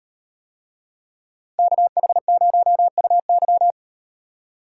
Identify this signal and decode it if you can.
Morse code, heard as a keyed tone, about 35 wpm, 700 Hz. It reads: KH0UY